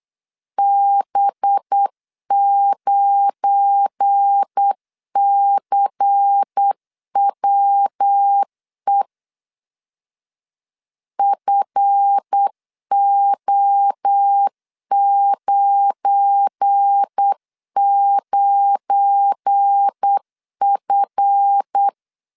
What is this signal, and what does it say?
Morse code, heard as a keyed tone, about 8 wpm, 800 Hz. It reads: B9CWE FO99F
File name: LEZIONI / 7 GF9Z / 7 GF9 8wpm 3545